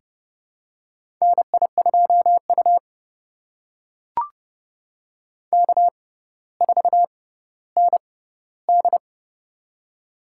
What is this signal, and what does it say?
Morse code, heard as a keyed tone, about 30 wpm, spaced as for 15 wpm, 700 Hz. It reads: NI2U K4ND
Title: USA Call Signs - 4 Characters - 6x Character Spacing - 30wpm